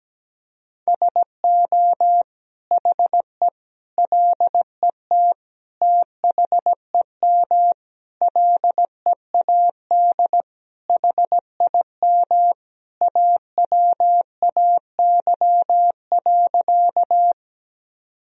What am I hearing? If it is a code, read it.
Morse code, heard as a keyed tone, about 17 wpm, 700 Hz. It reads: SO HE LET THEM LEAD HIM AWAY.
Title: Sentences from Top 1000 Words - Review Part 2 - 17wpm